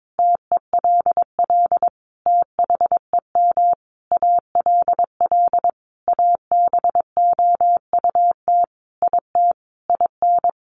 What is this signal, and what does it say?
Morse code, heard as a keyed tone, about 22 wpm, 700 Hz. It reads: TELL THEM ALL ABOUT IT IN
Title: Sentences from Top 900 Words 22wpm